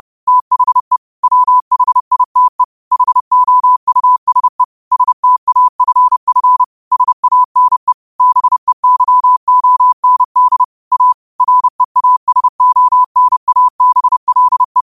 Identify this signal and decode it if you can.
Morse code, heard as a keyed tone, about 30 wpm, 1000 Hz. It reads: THE WHITE HOUSE STAFF SANE BEYOND A REASONABLE
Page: New quotes every day in morse code at 30 Words per minute.